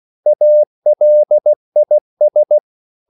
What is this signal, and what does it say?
Morse code, heard as a keyed tone, about 16 wpm, 600 Hz. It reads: ALIS